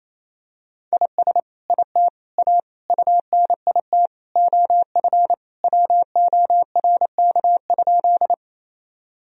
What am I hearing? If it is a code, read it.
Morse code, heard as a keyed tone, about 28 wpm, 700 Hz. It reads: IS IT A UNIT OF WORK?
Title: Sentences from Top 1000 Words - Review Part 2 - 28wpm